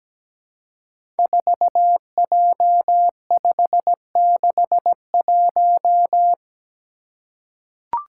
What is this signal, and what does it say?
Morse code, heard as a keyed tone, about 17 wpm, 700 Hz. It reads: 4J561
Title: Five Challenging Letter-Number Combinations 17wpm